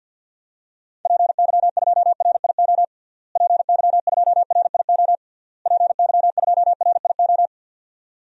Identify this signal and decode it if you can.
Morse code, heard as a keyed tone, about 51 wpm, 700 Hz. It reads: PY2RIO PY2RIO PY2RIO
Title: DXCC-100 Call Signs - 6 Characters - Repeated 3x - 50wpm